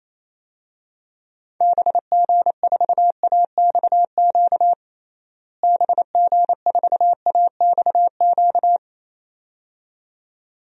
Morse code, heard as a keyed tone, about 28 wpm, 700 Hz. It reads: BG4AXQ BG4AXQ
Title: DXCC-100 Call Signs - 6 Characters - Repeated 2x - 28wpm